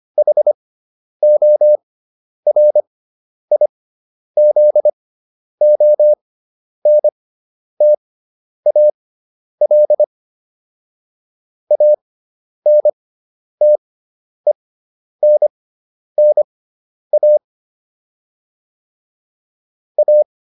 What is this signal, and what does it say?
Morse code, heard as a keyed tone, about 25 wpm, spaced as for 10 wpm, 600 Hz. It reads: HORIZONTAL ANTENNA A